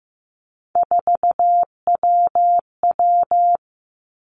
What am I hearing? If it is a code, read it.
Morse code, heard as a keyed tone, about 15 wpm, 700 Hz. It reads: 4WW
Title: Three Challenging Letter-Number Combinations 15wpm